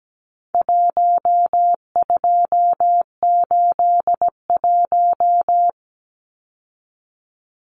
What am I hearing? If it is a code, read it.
Morse code, heard as a keyed tone, about 17 wpm, 700 Hz. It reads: 1281
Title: Numbers - 4-Digit 17wpm